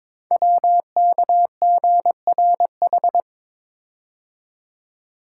Morse code, heard as a keyed tone, about 23 wpm, 700 Hz. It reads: WKGRH
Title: Code Groups - 5 Letters - 22wpm